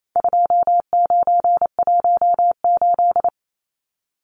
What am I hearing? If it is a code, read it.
Morse code, heard as a keyed tone, about 28 wpm, 700 Hz. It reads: 2918
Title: Numbers - 4-Digit 28wpm